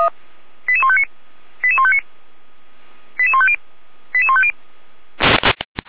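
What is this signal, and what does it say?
Five-tone selective call (ZVEI), heard as quick runs of five tones, tones of about 70 ms. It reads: 88159 88159 88150 88150
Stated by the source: ZVEI five-tone call 88245 received with an SDR.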